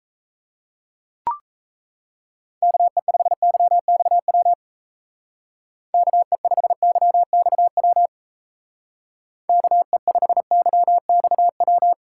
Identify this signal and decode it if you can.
Morse code, heard as a keyed tone, about 35 wpm, 700 Hz. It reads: KE5YXW KE5YXW KE5YXW
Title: Speed Racing - USA 2x3 Call Signs 28wpm